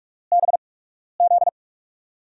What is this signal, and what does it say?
Morse code, heard as a keyed tone, about 44 wpm, spaced as for 12 wpm, 700 Hz. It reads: BZ
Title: USA Call Signs - 8x Character Spacing - 45wpm